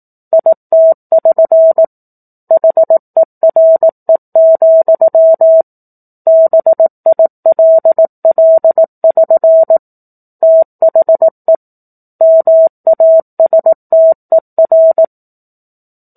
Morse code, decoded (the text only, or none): IT<SN> HERE, BILL<SN> THE MASTER